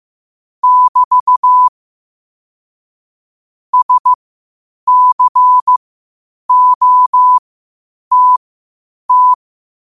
Morse code, heard as a keyed tone, about 15 wpm, spaced as for 8 wpm, 1000 Hz. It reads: <BT> SCOTT
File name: qotdfiles / 8wpm